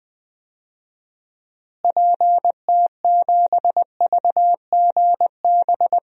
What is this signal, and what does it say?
Morse code, heard as a keyed tone, about 20 wpm, 700 Hz. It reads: PT7VGB
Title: DXCC-100 Call Signs - 6 Characters - Repeated 2x - 20wpm